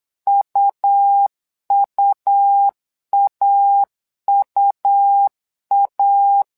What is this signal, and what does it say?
Morse code, heard as a keyed tone, about 8 wpm, 800 Hz. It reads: UUAUA
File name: LEZIONI / 2 UHDN / 2 U 8wpm 2028